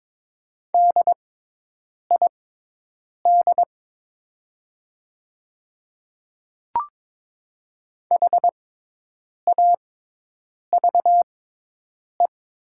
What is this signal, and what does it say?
Morse code, heard as a keyed tone, about 22 wpm, spaced as for 8 wpm, 690 Hz. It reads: DID HAVE
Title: 6x Character Spacing - Top 100 Words - 22wpm